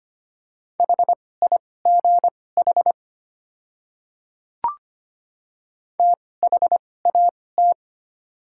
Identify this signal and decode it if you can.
Morse code, heard as a keyed tone, about 25 wpm, spaced as for 18 wpm, 700 Hz. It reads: HIGH THAT
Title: 2x Character Spacing - Top 300 Words - 25wpm